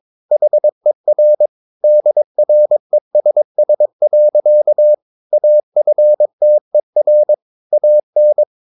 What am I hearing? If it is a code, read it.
Morse code, heard as a keyed tone, about 22 wpm, 600 Hz. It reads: HER DRESS. AFTER AN
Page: Romanen Never af Ken Follet på engelsk.